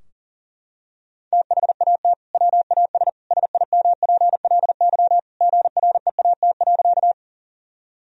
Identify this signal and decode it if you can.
Morse code, heard as a keyed tone, about 37 wpm, 700 Hz. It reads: THAT WAS SIMPLY GREAT.
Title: Sentences from Top 1000 Words - Review Part 4 - 40wpm